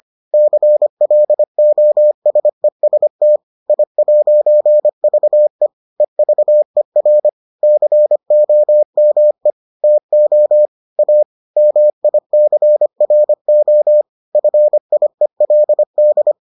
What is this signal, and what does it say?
Morse code, heard as a keyed tone, about 25 wpm, 600 Hz. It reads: CLOSEST I'VE EVER COME TO A MICRO FIELD